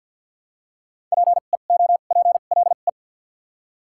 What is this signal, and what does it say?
Morse code, heard as a keyed tone, about 52 wpm, spaced as for 36 wpm, 700 Hz. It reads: PEOPLE
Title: 2x Character Spacing - Top 300 Words - 50wpm